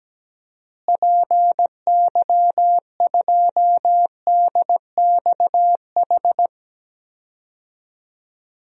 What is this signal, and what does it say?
Morse code, heard as a keyed tone, about 17 wpm, 700 Hz. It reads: PY2DXH
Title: DXCC-100 Call Signs - 6 Characters - Repeated 3x - 17wpm